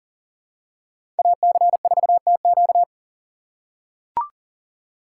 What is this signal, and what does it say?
Morse code, heard as a keyed tone, about 40 wpm, 700 Hz. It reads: AC4TQ
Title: USA Call Signs - 5 Characters - 6x Character Spacing - 40wpm